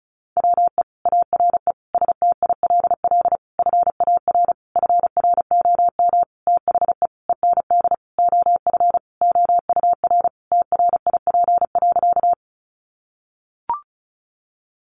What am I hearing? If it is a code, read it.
Morse code, heard as a keyed tone, about 34 wpm, 700 Hz. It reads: WE ARE STILL FAR FROM THE END OF OUR TRIP.
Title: Sentences from Top 1300 Words 35wpm